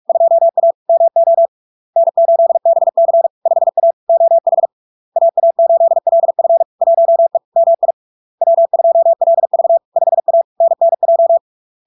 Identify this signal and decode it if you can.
Morse code, heard as a keyed tone, about 46 wpm, 670 Hz. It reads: 2A MO N8BC 5A OH AA8LF 1E MI W2LV 5A NNJ
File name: LARK-FD-2022-Log-in-CW-45wpm-670-Hz